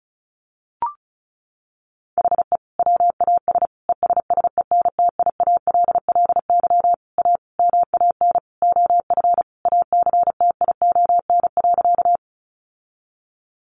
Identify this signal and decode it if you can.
Morse code, heard as a keyed tone, about 34 wpm, 700 Hz. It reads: HE WAS ESSENTIALLY A MAN OF ACTION.
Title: Sentences from Top 1300 Words 35wpm